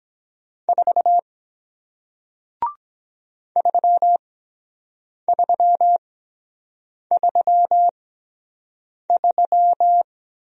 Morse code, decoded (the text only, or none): 4 3333